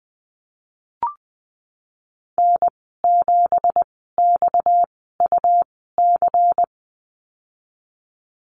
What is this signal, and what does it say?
Morse code, heard as a keyed tone, about 20 wpm, spaced as for 14 wpm, 700 Hz. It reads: N7XUC
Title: 238 Lesson 38g - Letter q - Call Signs 20wpm